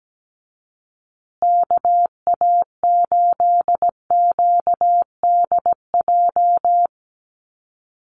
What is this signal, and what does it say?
Morse code, heard as a keyed tone, about 17 wpm, 700 Hz. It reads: KA8QDJ
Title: USA Call Signs Repeated 3x 17wpm